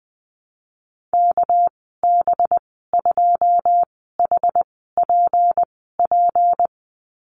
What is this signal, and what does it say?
Morse code, heard as a keyed tone, about 20 wpm, spaced as for 14 wpm, 700 Hz. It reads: KB2HPP
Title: 213 Lesson 34g - Number 8 - Call Signs 20wpm